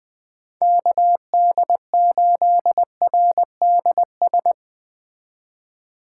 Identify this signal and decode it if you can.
Morse code, heard as a keyed tone, about 21 wpm, 700 Hz. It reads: KD8RDS